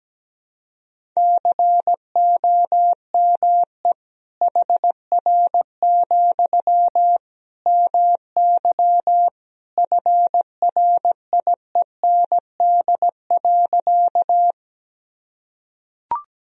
Morse code, heard as a keyed tone, about 17 wpm, 700 Hz. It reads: COME HR, MY FRIEND.
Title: Sentences from Top 300 Words and Abbreviations 17wpm